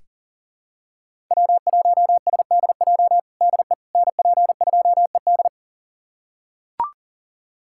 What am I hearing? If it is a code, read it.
Morse code, heard as a keyed tone, about 40 wpm, 700 Hz. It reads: W1SDJ DE NP2ED
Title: USA Callsign DE Callsign - 5 Characters - 40wpm